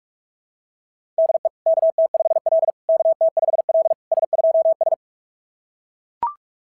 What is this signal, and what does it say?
Morse code, heard as a keyed tone, about 41 wpm, 660 Hz. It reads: DE KT5L KT5L S2S
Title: SOTA QSO Practice 45wpm